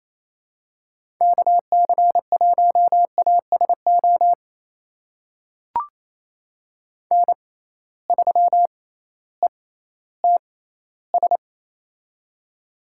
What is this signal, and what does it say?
Morse code, heard as a keyed tone, about 28 wpm, spaced as for 15 wpm, 700 Hz. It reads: KC1ASO N3ETS
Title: USA Call Signs - 6x Character Spacing - 28wpm